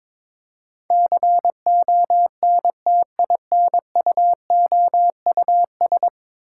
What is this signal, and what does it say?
Morse code, heard as a keyed tone, about 22 wpm, 700 Hz. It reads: CONTINUOUS
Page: morse-code-ninja-practice-22wpm